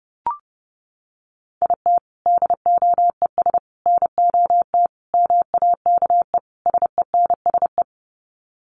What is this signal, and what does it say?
Morse code, heard as a keyed tone, about 28 wpm, 700 Hz. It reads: IT DOES NOT MAKE SENSE
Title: Sets of 5 Words Encore2 30wpm